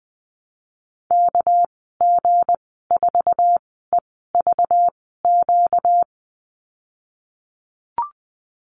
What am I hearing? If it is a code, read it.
Morse code, heard as a keyed tone, about 20 wpm, spaced as for 15 wpm, 700 Hz. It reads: KG4EVQ
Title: 238 Lesson 38g - Letter q - Call Signs 20wpm